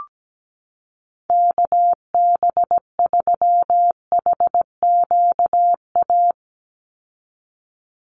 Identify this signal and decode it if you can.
Morse code, heard as a keyed tone, about 17 wpm, 700 Hz. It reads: KB3HQA
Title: USA Call Signs 17wpm